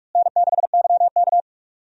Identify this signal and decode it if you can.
Morse code, heard as a keyed tone, about 46 wpm, 700 Hz. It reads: N6YK
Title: USA Call Signs - 4 Characters - Repeated 3x - 45wpm